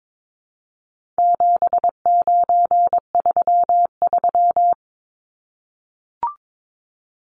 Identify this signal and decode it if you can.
Morse code, heard as a keyed tone, about 22 wpm, 700 Hz. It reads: 7933
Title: Numbers - 4-Digit 22wpm